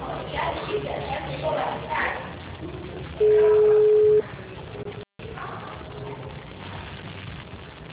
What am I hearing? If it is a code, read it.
Ringback tone (Europe, 425 Hz), with speech.
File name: out- -2008- - - .15645